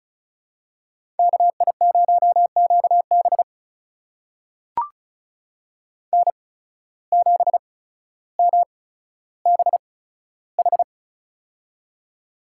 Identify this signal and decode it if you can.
Morse code, heard as a keyed tone, about 34 wpm, spaced as for 15 wpm, 700 Hz. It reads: KI0QB N7MBH
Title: USA Call Signs - 5 Characters - 8x Character Spacing - 35wpm